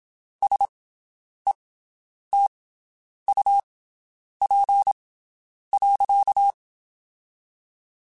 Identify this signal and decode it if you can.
Morse code, heard as a keyed tone, about 27 wpm, spaced as for 9 wpm, 800 Hz. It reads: SETUP.
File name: (spaced) / QSOs Spaced-mp3s / WD0BGP 1 @25 sp